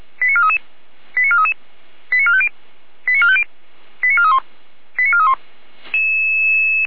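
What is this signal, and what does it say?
Five-tone selective call (ZVEI), heard as quick runs of five tones, tones of about 70 ms. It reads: 89433 89433 89450 89450 89431 89431
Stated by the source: ZVEI five-tone selective call received by SDR.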